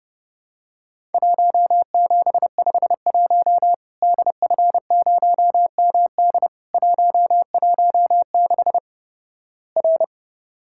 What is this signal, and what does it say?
Morse code, heard as a keyed tone, about 30 wpm, 700 Hz. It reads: 1751 DF0MD 116 R
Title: Worked All Europe DX Contest - QTC Exchange - 30wpm